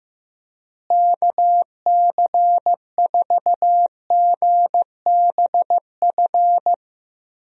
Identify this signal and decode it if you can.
Morse code, heard as a keyed tone, about 15 wpm, 700 Hz. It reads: KC4GBF
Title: USA Call Signs - 6 Characters - Repeated 3x - 15wpm